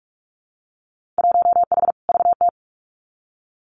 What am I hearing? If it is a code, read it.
Morse code, heard as a keyed tone, about 46 wpm, 700 Hz. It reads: 1H VT
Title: Winter Field Day Exchanges 45wpm